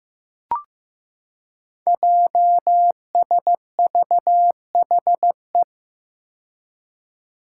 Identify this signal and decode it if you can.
Morse code, heard as a keyed tone, about 15 wpm, 700 Hz. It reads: JSVHE
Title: Five Challenging Letter-Number Combinations 15wpm